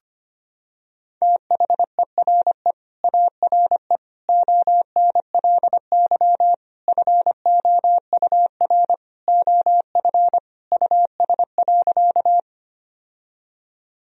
Morse code, heard as a keyed tone, about 25 wpm, 700 Hz. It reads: THERE ARE ONLY FOUR OF US.
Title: Sentences from Top 1000 Words - Review Part 1 - 25wpm